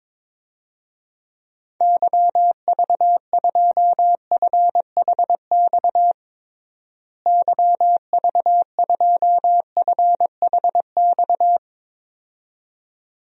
Morse code, heard as a keyed tone, about 22 wpm, 700 Hz. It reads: YV2FHX YV2FHX
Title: DXCC-100 Call Signs - 6 Characters - Repeated 2x - 22wpm